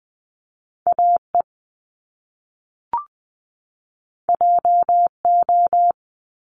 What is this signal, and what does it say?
Morse code, heard as a keyed tone, about 20 wpm, 700 Hz. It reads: AE JO
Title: Top Two Letter Combinations in Words 20wpm